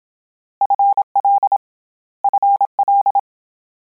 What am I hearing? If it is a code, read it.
Morse code, heard as a keyed tone, about 27 wpm, 800 Hz. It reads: FL FL